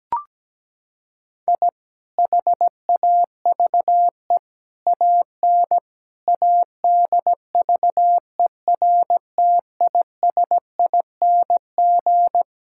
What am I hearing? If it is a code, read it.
Morse code, heard as a keyed tone, about 17 wpm, 700 Hz. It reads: I HAVE AN ADVERTISING
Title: Sentences from Top 1000 Words - Review Part 6 - 17wpm